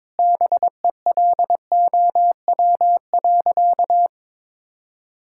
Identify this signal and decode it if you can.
Morse code, heard as a keyed tone, about 22 wpm, 700 Hz. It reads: BELOW.
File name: Sentences with 9-Word Phrases - 22wpm